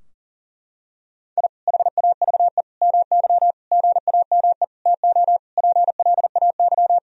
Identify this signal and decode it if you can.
Morse code, heard as a keyed tone, about 40 wpm, 700 Hz. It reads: I HAVE MY GAME TO PLAY
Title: Sentences from Top 1000 Words - Review Part 3 - 40wpm